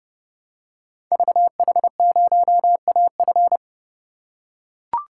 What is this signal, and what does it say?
Morse code, heard as a keyed tone, about 30 wpm, 700 Hz. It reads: VH0AF
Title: Code Groups - 5 Letter:Numbers - 30wpm